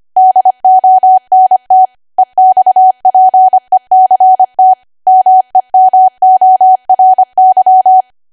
Morse code, heard as a keyed tone, about 25 wpm, 750 Hz. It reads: DONT EXPECT MEMORY
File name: cwa / W1AW-Level3 / 25WPM